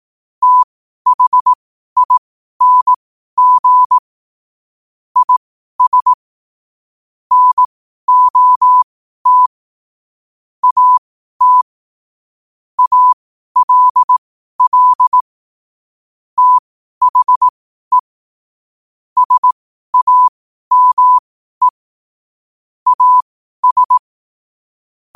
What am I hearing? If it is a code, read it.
Morse code, heard as a keyed tone, about 18 wpm, spaced as for 12 wpm, 1000 Hz. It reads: THING IS NOT AT ALL THE SAME AS
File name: qotdfiles / 12wpm